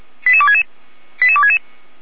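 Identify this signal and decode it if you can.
Five-tone selective call (ZVEI), heard as quick runs of five tones, tones of about 70 ms. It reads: 88266 88266
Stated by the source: Five-tone ZVEI selective call 88129, received on SDR, recorded by feuerwehr-orlamuende (fire brigade Orlamünde).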